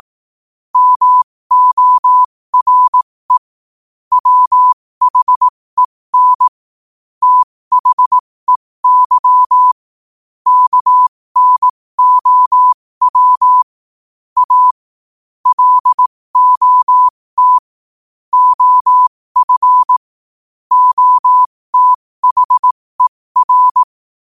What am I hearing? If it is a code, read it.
Morse code, heard as a keyed tone, about 18 wpm, spaced as for 15 wpm, 1000 Hz. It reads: MORE WHEN THEY KNOW A LOT OF OTHER
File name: qotdfiles / 15wpm